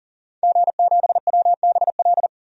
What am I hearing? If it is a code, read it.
Morse code, heard as a keyed tone, about 40 wpm, 700 Hz. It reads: G7WBL
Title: DXCC-100 Call Signs - 5 Characters - 40wpm